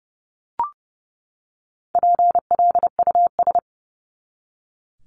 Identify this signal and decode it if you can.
Morse code, heard as a keyed tone, about 30 wpm, 700 Hz. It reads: PLUS
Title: Top 1001-1250 Words 30wpm